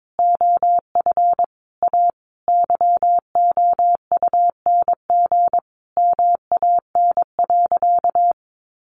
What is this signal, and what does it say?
Morse code, heard as a keyed tone, about 22 wpm, 700 Hz. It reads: OF A YOUNG MAN.
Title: Sentences from Top 500 Words 22wpm